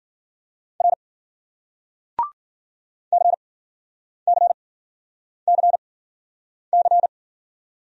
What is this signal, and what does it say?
Morse code, heard as a keyed tone, about 51 wpm, spaced as for 9 wpm, 700 Hz. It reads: R CCCC